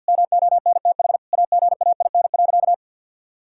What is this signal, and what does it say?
Morse code, heard as a keyed tone, about 50 wpm, 700 Hz. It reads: MONTH AGAIN.
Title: Sentences from Top 1000 Words - Review Part 5 - 50wpm